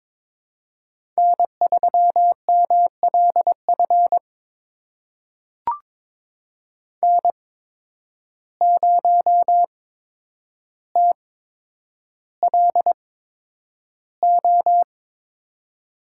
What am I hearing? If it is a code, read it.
Morse code, heard as a keyed tone, about 22 wpm, spaced as for 9 wpm, 700 Hz. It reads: N3MLF N0TLO
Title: USA Call Signs - 5 Characters - 8x Character Spacing - 22wpm